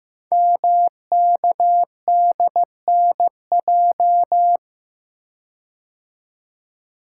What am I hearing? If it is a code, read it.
Morse code, heard as a keyed tone, about 15 wpm, 700 Hz. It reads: MKDNJ
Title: Code Groups - 5 Letters - 15wpm